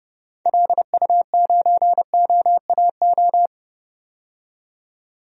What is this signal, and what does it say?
Morse code, heard as a keyed tone, about 32 wpm, 700 Hz. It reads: LU9OAO
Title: DXCC-100 Call Signs - 6 Characters - 30wpm